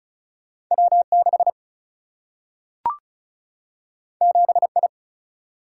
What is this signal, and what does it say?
Morse code, heard as a keyed tone, about 36 wpm, 700 Hz. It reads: W 6 7 I
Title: Single-Letter Single-Number 35wpm